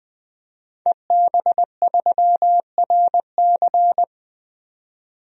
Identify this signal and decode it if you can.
Morse code, heard as a keyed tone, about 20 wpm, 700 Hz. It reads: EB3RC
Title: DXCC-100 Call Signs - 5 Characters - 20wpm